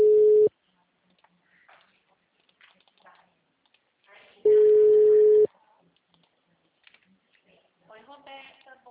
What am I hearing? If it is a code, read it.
Ringback tone (Europe, 425 Hz), with speech.